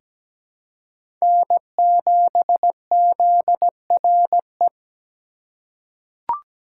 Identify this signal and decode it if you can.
Morse code, heard as a keyed tone, about 17 wpm, 700 Hz. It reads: N7ZRE
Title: USA Call Signs - 5 Characters - 2x Character Spacing - 17wpm